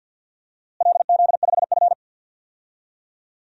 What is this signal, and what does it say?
Morse code, heard as a keyed tone, about 52 wpm, 700 Hz. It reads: LZ5F
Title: DXCC-100 Call Signs - 4 Characters - Repeated 3x - 50wpm